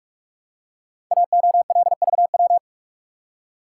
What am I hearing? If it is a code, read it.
Morse code, heard as a keyed tone, about 44 wpm, 700 Hz. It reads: AOLVW
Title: Code Groups - 5 Letters - 45wpm